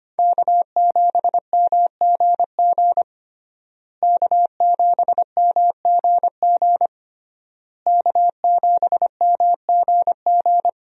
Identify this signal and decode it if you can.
Morse code, heard as a keyed tone, about 25 wpm, 700 Hz. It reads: K7MGG K7MGG K7MGG
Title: USA Call Signs - 5 Characters - Repeated 3x - 25wpm